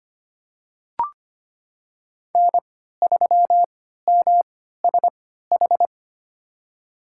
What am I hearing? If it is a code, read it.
Morse code, heard as a keyed tone, about 25 wpm, spaced as for 14 wpm, 700 Hz. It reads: N3MSH
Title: USA Call Signs - 3x Character Spacing - 25wpm